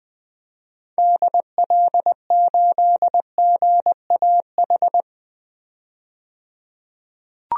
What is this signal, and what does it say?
Morse code, heard as a keyed tone, about 21 wpm, 700 Hz. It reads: DL8GAH
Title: DXCC-100 Call Signs - 6 Characters - 20wpm